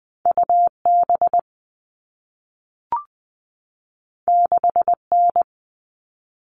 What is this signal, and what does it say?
Morse code, heard as a keyed tone, about 20 wpm, 700 Hz. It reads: UB 6N
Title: All Cut Numbers - 2-Digit 20wpm